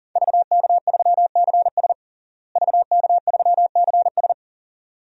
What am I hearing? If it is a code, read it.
Morse code, heard as a keyed tone, about 40 wpm, 700 Hz. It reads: VK3CS VK3CS
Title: DXCC-100 Call Signs - 5 Characters - Repeated 2x - 40wpm